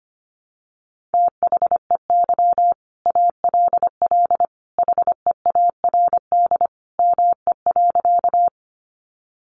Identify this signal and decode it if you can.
Morse code, heard as a keyed tone, about 25 wpm, 700 Hz. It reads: THEY ALL HEARD ME.
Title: Sentences from Top 400 Words 25wpm